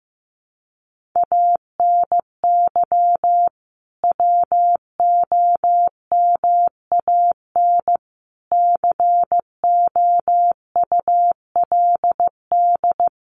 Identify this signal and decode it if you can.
Morse code, heard as a keyed tone, about 15 wpm, 700 Hz. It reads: ANY WOMAN COULD
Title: Sentences from Top 1100 Words 15wpm